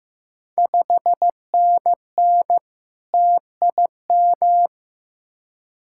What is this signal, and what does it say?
Morse code, heard as a keyed tone, about 15 wpm, 700 Hz. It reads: 5NN TIM
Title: Ontario QSO Party Exchange 15wpm